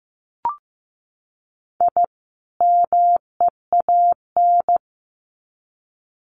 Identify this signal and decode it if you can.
Morse code, heard as a keyed tone, about 15 wpm, 700 Hz. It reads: I MEAN
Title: Sets of 2 Words 15wpm